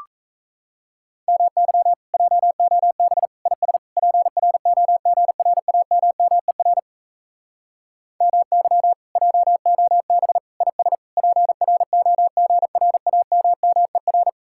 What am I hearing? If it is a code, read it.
Morse code, heard as a keyed tone, about 38 wpm, 700 Hz. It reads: MY JOB IS PROGRAMMER MY JOB IS PROGRAMMER